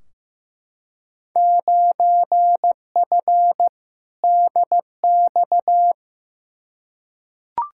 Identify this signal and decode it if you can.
Morse code, heard as a keyed tone, about 15 wpm, 700 Hz. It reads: 9F DX
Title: Field Day Exchanges Encore - 15wpm